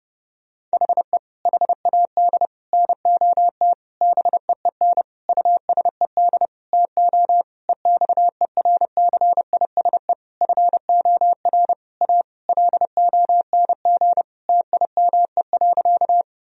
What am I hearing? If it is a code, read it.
Morse code, heard as a keyed tone, about 30 wpm, 700 Hz. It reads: HE HAD NOT BEEN USED TO EXERCISE FOR A LONG TIME.
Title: Sentences from Top 1000 Words - Review Part 4 - 30wpm